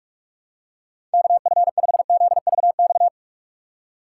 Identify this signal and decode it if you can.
Morse code, heard as a keyed tone, about 46 wpm, 700 Hz. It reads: KF5ZVX
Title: USA Call Signs - 6 Characters - 3x Character Spacing - 45wpm